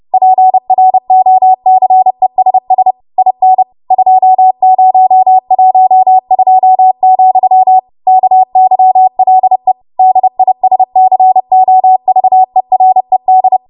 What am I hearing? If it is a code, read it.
Morse code, heard as a keyed tone, about 30 wpm, 750 Hz. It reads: PROCESS IN 2012, KYLE DISCOVERED
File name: Morse / 30WPM